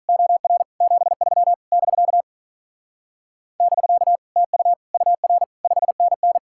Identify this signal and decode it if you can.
Morse code, heard as a keyed tone, about 45 wpm, 700 Hz. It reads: OR 73 <BK> <BK> TU UR 5NN